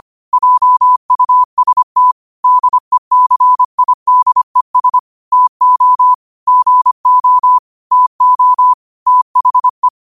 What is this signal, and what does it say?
Morse code, heard as a keyed tone, about 25 wpm, 1000 Hz. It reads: JUST DECIDES TO GO TO THE
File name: qotdfiles / 25wpm